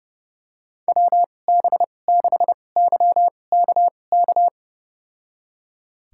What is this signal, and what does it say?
Morse code, heard as a keyed tone, about 30 wpm, spaced as for 22 wpm, 700 Hz. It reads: WB6YKK
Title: USA Call Signs - 6 Characters - 2x Character Spacing - 30wpm